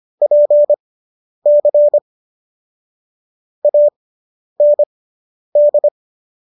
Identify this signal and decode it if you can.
Morse code, heard as a keyed tone, about 25 wpm, spaced as for 10 wpm, 590 Hz. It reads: PC AND